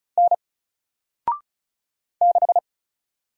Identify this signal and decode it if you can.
Morse code, heard as a keyed tone, about 35 wpm, 700 Hz. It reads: N 6
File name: Single-Letter-Number-35wpm